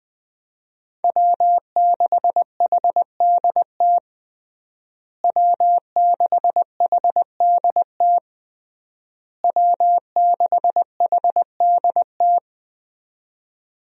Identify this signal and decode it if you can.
Morse code, heard as a keyed tone, about 20 wpm, 700 Hz. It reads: W6HDT W6HDT W6HDT
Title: USA Call Signs - 5 Characters - Repeated 3x - 20wpm